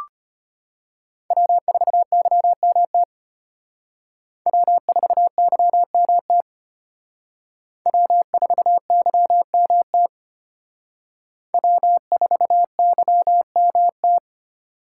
Morse code, heard as a keyed tone, about 32 wpm, 700 Hz. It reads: W4YMT W4YMT W4YMT W4YMT